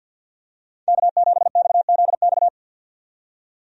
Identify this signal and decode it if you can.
Morse code, heard as a keyed tone, about 50 wpm, 700 Hz. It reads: K7XZX
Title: USA Call Signs - 5 Characters - 50wpm